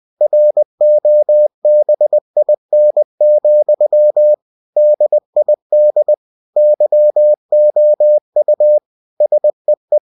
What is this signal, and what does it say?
Morse code, heard as a keyed tone, about 20 wpm, 600 Hz. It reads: ROBIN, DID YOU SEE
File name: cw-books / winnie-the-pooh / winnie-20wpm-page-0087